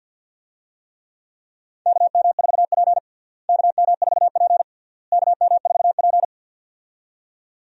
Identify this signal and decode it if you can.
Morse code, heard as a keyed tone, about 52 wpm, 700 Hz. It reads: KM4P KM4P KM4P
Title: USA Call Signs - 4 Characters - Repeated 3x - 50wpm